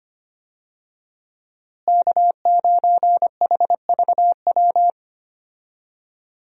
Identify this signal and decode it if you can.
Morse code, heard as a keyed tone, about 25 wpm, 700 Hz. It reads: K9HVW
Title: USA Call Signs - 5 Characters - Repeated 2x - 25wpm